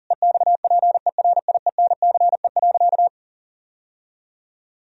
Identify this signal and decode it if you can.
Morse code, heard as a keyed tone, about 40 wpm, 700 Hz. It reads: EXPERIENCE.
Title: Sentences from Top 1000 Words - Review Part 3 - 40wpm